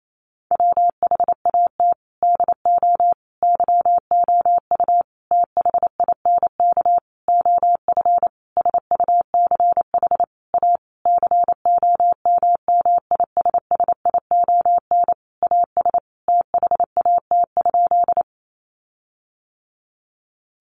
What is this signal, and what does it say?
Morse code, heard as a keyed tone, about 28 wpm, 700 Hz. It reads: WHAT DO YOU THINK OF SUCH A COMMISSION AS THAT?
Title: Sentences from Top 1400 Words 28wpm